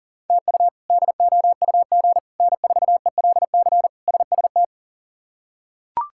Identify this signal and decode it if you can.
Morse code, heard as a keyed tone, about 37 wpm, 700 Hz. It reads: TU DOUG N4ELC SST
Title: K1USN SST QSO Practice - 40wpm